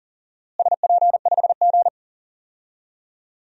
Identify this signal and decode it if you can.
Morse code, heard as a keyed tone, about 40 wpm, 700 Hz. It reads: SP5G
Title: DXCC-100 Call Signs - 4 Characters - Repeated 3x - 40wpm